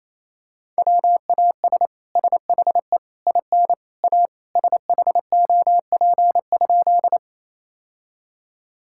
Morse code, heard as a keyed tone, about 27 wpm, 700 Hz. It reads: WAS SHE IN A SHOP?
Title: Sentences from Top 1600 Words 28wpm